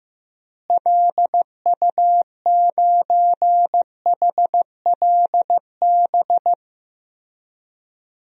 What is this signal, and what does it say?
Morse code, heard as a keyed tone, about 15 wpm, 700 Hz. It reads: LU9HLB
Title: DXCC-100 Call Signs - 6 Characters - Repeated 2x - 15wpm